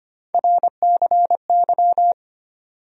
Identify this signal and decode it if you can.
Morse code, heard as a keyed tone, about 25 wpm, 700 Hz. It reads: RCY
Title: USA Call Sign Postfixes - 3 Letters 25wpm